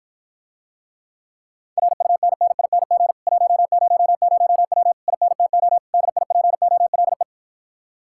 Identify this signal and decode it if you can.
Morse code, heard as a keyed tone, about 54 wpm, 700 Hz. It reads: RUNNING 100W INTO DIPOLE